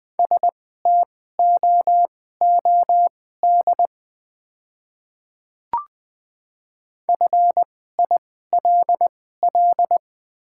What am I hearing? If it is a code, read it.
Morse code, heard as a keyed tone, about 20 wpm, spaced as for 14 wpm, 690 Hz. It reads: STOOD FILL
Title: 2x Character Spacing - Top 500 Words - 20wpm